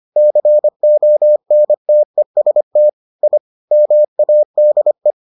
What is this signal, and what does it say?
Morse code, heard as a keyed tone, about 25 wpm, 600 Hz. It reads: CONTEST I MADE